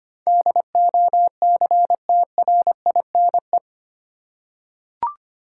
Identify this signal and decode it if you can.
Morse code, heard as a keyed tone, about 25 wpm, 700 Hz. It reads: DOCTRINE
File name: 8 Letter Words - First 3 Letters Given - 25wpm